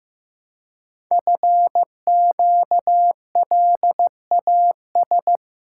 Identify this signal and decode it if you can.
Morse code, heard as a keyed tone, about 15 wpm, 700 Hz. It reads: FQLAS
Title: Code Groups - 5 Letters - 15wpm